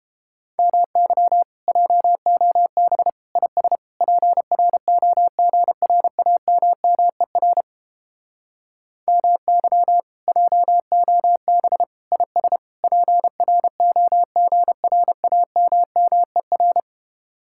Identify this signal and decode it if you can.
Morse code, heard as a keyed tone, about 32 wpm, 700 Hz. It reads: MY JOB IS PROGRAMMER MY JOB IS PROGRAMMER